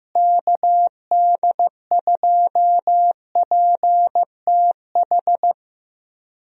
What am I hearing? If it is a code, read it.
Morse code, heard as a keyed tone, about 15 wpm, 700 Hz. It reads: KD2PTH
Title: USA Call Signs - 6 Characters - Repeated 3x - 15wpm